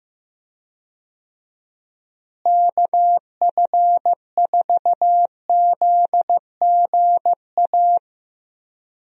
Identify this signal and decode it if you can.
Morse code, heard as a keyed tone, about 15 wpm, 700 Hz. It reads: KF4ZGA
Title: USA Call Signs - 6 Characters - Repeated 3x - 15wpm